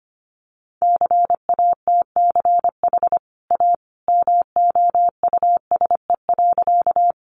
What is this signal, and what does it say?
Morse code, heard as a keyed tone, about 25 wpm, 700 Hz. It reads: CATCH A MOUSE.
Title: Sentences from Top 1500 Words 25wpm